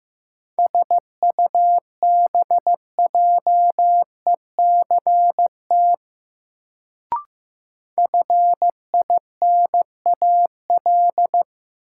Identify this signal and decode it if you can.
Morse code, heard as a keyed tone, about 15 wpm, 700 Hz. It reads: SUBJECT FINAL
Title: Sentences from Top 800 Words - New Word Drill - 15wpm